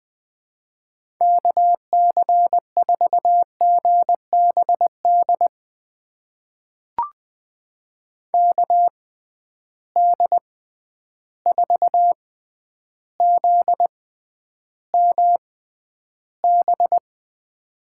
Morse code, heard as a keyed tone, about 20 wpm, spaced as for 10 wpm, 700 Hz. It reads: KC4GBD KD4ZMB
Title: USA Call Signs - 6 Characters - 6x Character Spacing - 20wpm